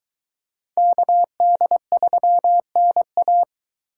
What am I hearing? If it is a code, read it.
Morse code, heard as a keyed tone, about 22 wpm, 700 Hz. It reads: KD3NA